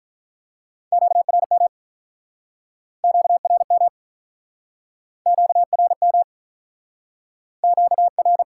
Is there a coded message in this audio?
Yes: Morse code from 0.9 to 8.4 s, heard as a keyed tone, about 48 wpm, 700 Hz. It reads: QRM QRM QRM QR